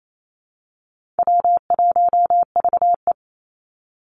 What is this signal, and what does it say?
Morse code, heard as a keyed tone, about 28 wpm, 700 Hz. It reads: W1VE
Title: Call Signs - 28wpm